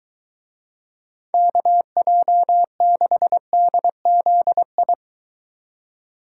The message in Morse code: KJ6DZI

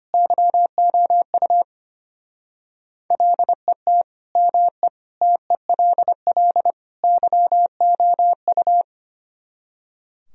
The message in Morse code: YOU LET ME TELL YOU

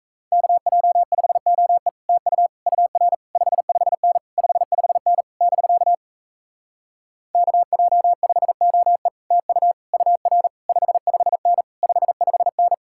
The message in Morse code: KJ5OE TU UR 55N 55N <BK> KJ5OE TU UR 55N 55N

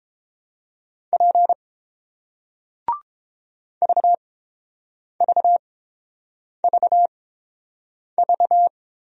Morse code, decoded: P VVVV

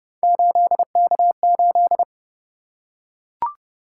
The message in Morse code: 8K8

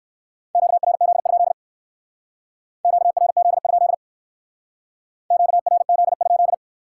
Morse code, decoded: QRZ? QRZ? QRZ?

